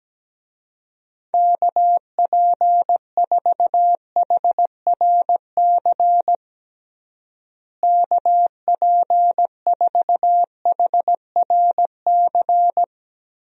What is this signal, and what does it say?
Morse code, heard as a keyed tone, about 17 wpm, 700 Hz. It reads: KP4HRC KP4HRC
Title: USA Call Signs - 6 Characters - Repeated 2x - 17wpm